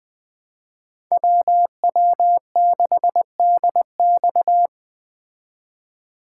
Morse code, heard as a keyed tone, about 19 wpm, 700 Hz. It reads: WW6DX